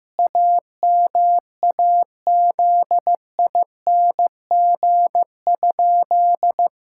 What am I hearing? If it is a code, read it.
Morse code, heard as a keyed tone, about 15 wpm, 700 Hz. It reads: AMAZING?
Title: Pangram Challenge v2 - 15wpm